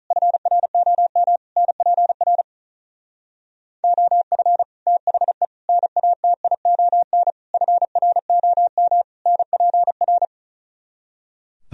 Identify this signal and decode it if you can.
Morse code, heard as a keyed tone, about 46 wpm, 700 Hz. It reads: FROM NPR OF THE NATION FROM NPR